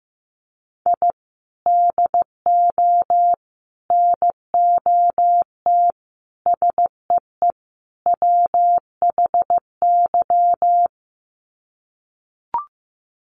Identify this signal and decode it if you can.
Morse code, heard as a keyed tone, about 15 wpm, 700 Hz. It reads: I DO NOT SEE WHY
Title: Sets of 5 Words Encore 15wpm